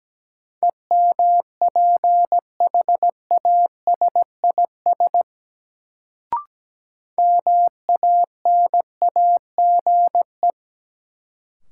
Morse code, heard as a keyed tone, about 17 wpm, 700 Hz. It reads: EMPHASIS MANAGE
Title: Sentences from Top 1400 Words - New Word Drill Alt - 17wpm